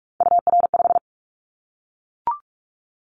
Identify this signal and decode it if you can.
Morse code, heard as a keyed tone, about 44 wpm, 700 Hz. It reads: UR5
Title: Callsign Prefixes for Top 25 DXCC - Repeated 3x - 45wpm